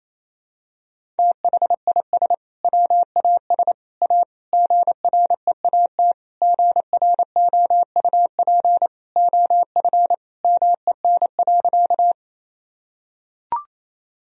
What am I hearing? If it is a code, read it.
Morse code, heard as a keyed tone, about 28 wpm, 700 Hz. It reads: THIS WAS A GREAT GROUP OF MEN.
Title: Sentences from Top 1000 Words - Review Part 1 - 28wpm